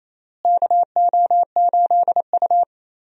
Morse code, heard as a keyed tone, about 28 wpm, 700 Hz. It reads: KO8U
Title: USA Call Signs - 4 Characters - Repeated 3x - 28wpm